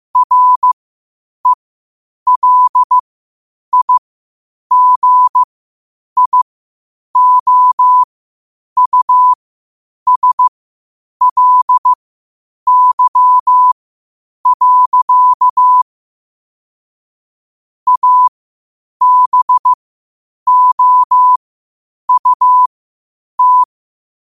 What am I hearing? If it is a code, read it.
Morse code, heard as a keyed tone, about 15 wpm, spaced as for 8 wpm, 1000 Hz. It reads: RELIGIOUSLY. ABOUT
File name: qotdfiles / 8wpm